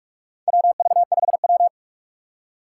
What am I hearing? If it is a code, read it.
Morse code, heard as a keyed tone, about 46 wpm, 700 Hz. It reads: WV5W
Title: USA Call Signs - 4 Characters - 4x Character Spacing - 45wpm